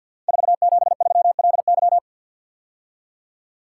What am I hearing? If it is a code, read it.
Morse code, heard as a keyed tone, about 53 wpm, 700 Hz. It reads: 473LY
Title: Code Groups - 5 Letter:Numbers - 50wpm